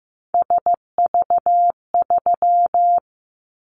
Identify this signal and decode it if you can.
Morse code, heard as a keyed tone, about 15 wpm, 700 Hz. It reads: SV3
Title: Callsign Prefixes for Top 50 DXCC - 15wpm